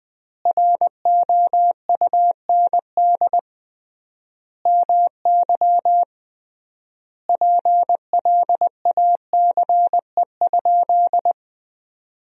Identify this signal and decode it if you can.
Morse code, heard as a keyed tone, about 20 wpm, 700 Hz. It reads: ROUND MY PLACE?
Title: 3x Word-Spacing - Sentences from Top 200 Words - 20wpm